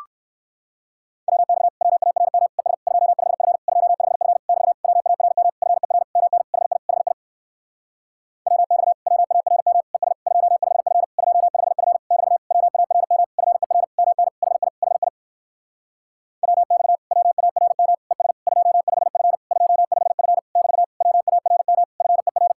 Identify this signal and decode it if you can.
Morse code, heard as a keyed tone, about 59 wpm, 700 Hz. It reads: WX WARM ES 25F 25F <BT> WARM FER MN HI HI WX WARM ES 25F 25F <BT> WARM FER MN HI HI WX WARM ES 25F 25F <BT> WARM FER